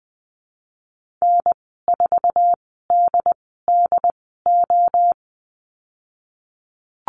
Morse code, heard as a keyed tone, about 19 wpm, spaced as for 14 wpm, 700 Hz. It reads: N4DDO
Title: 088 Lesson 15g - Number 5 - Call Signs 20wpm